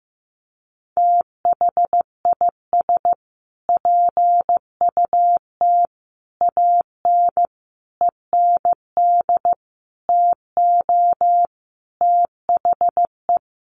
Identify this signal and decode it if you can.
Morse code, heard as a keyed tone, about 15 wpm, 700 Hz. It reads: THIS PUT AN END TO THE
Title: Sentences from Top 1100 Words 15wpm